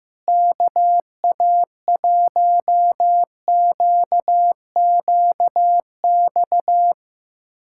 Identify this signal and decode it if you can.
Morse code, heard as a keyed tone, about 15 wpm, 700 Hz. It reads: KA1QQX
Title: USA Callsign DE Callsign - 6 Characters - 15wpm